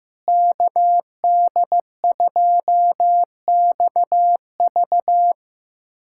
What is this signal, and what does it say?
Morse code, heard as a keyed tone, about 16 wpm, 700 Hz. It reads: KD2XV
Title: USA Callsign DE Callsign - 5 Characters - 15wpm